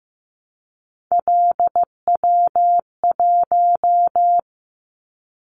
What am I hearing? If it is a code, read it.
Morse code, heard as a keyed tone, about 15 wpm, 700 Hz. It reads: LW1
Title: Callsign Prefixes for Top 50 DXCC - 15wpm